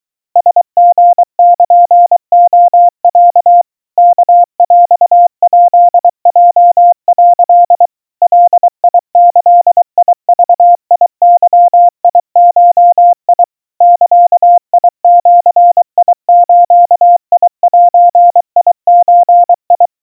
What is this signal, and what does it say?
Morse code (Japanese Wabun code), heard as a keyed tone, about 24 wpm, 700 Hz. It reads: ラリルレロ ワヰヱヲ」 ガギグゲゴ ザジズゼゾ
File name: kawa / Morse sample / aiueo 80 cpm 700 Hz Crispy